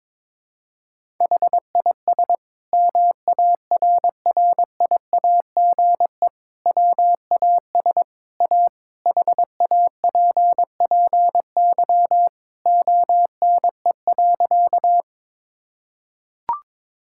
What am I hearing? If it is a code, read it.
Morse code, heard as a keyed tone, about 22 wpm, 700 Hz. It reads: HIS MARRIAGE WAS A HAPPY ONE.
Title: Sentences from Top 1000 Words - Review Part 5 - 22wpm